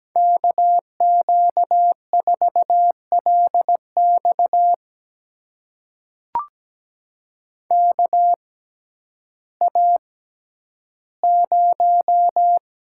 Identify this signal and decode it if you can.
Morse code, heard as a keyed tone, about 17 wpm, spaced as for 10 wpm, 700 Hz. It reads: KQ4LX KA0
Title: USA Call Signs - 6x Character Spacing - 17wpm